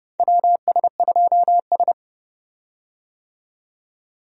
Morse code, heard as a keyed tone, about 30 wpm, 700 Hz. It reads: WS2S
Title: USA Call Signs - 4 Characters - 30wpm